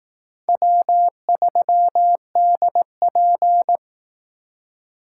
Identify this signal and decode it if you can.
Morse code, heard as a keyed tone, about 18 wpm, 700 Hz. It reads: W3DP
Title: Speed Racing - USA 1x2 Call Signs 15wpm